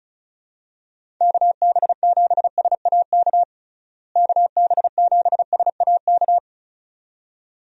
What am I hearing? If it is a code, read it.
Morse code, heard as a keyed tone, about 36 wpm, 700 Hz. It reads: KB7SAK KB7SAK
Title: USA Call Signs - 6 Characters - Repeated 2x - 35wpm